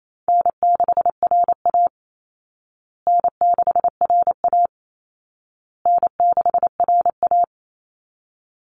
Morse code, heard as a keyed tone, about 28 wpm, 700 Hz. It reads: N6RA N6RA N6RA
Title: Easy Call Signs Repeated 3x - 28wpm